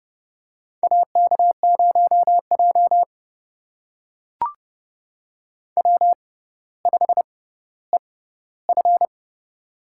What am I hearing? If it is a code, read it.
Morse code, heard as a keyed tone, about 30 wpm, spaced as for 15 wpm, 700 Hz. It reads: AK0J W5EF
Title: USA Call Signs - 4 Characters - 6x Character Spacing - 30wpm